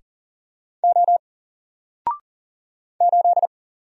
Morse code, heard as a keyed tone, about 40 wpm, 700 Hz. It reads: O 8
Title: Single Character (Letter:Number:Punctuation:Prosign) 40wpm